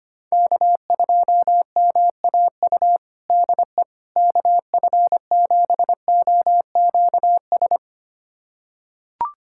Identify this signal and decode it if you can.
Morse code, heard as a keyed tone, about 25 wpm, 700 Hz. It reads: K2MAU DE KF7OQS
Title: USA Callsign DE Callsign 25wpm